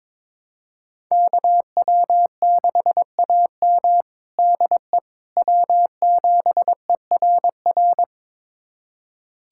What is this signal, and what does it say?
Morse code, heard as a keyed tone, about 21 wpm, 700 Hz. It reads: KW6AM DE W7ERR
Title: USA Callsign DE Callsign - 5 Characters - 22wpm